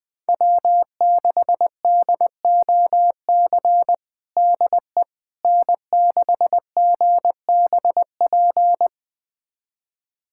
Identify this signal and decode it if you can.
Morse code, heard as a keyed tone, about 21 wpm, 700 Hz. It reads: W6DOC DE N6GBP
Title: USA Callsign DE Callsign - 5 Characters - 20wpm